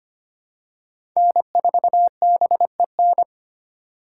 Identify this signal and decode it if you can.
Morse code, heard as a keyed tone, about 25 wpm, 700 Hz. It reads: N4BEN
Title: USA Call Signs - 5 Characters - 2x Character Spacing - 25wpm